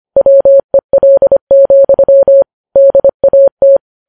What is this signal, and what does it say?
Morse code, heard as a keyed tone, about 25 wpm, 550 Hz. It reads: WEL, DAT
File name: Files Web tls / Morse on9fv / morse25kps